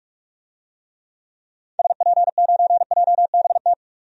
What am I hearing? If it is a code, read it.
Morse code, heard as a keyed tone, about 46 wpm, 700 Hz. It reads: SP9JBT
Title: DXCC-100 Call Signs - 6 Characters - Repeated 3x - 45wpm